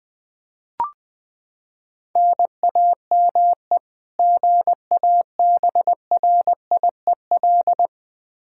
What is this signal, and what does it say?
Morse code, heard as a keyed tone, about 21 wpm, 700 Hz. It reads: NAME GABRIEL
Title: K1USN SST QSO Practice 20wpm